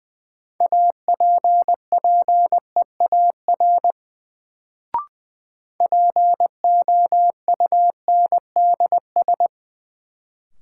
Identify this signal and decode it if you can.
Morse code, heard as a keyed tone, about 20 wpm, 700 Hz. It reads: APPEAR POUNDS
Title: Sentences from Top 1000 Words - Review Part 2 - New Word Drill - 20wpm